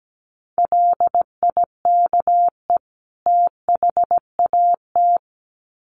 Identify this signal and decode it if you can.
Morse code, heard as a keyed tone, about 17 wpm, 700 Hz. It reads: LIKE THAT
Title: Sets of 2 Words 17wpm